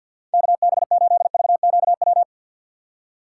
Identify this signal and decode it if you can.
Morse code, heard as a keyed tone, about 50 wpm, 700 Hz. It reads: KB8VQW
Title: USA Call Signs - 2x Character Spacing - 50wpm